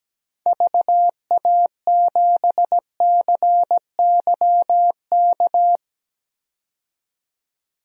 Morse code, heard as a keyed tone, about 17 wpm, 700 Hz. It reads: VA7CYK
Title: DXCC-100 Call Signs - 6 Characters - 17wpm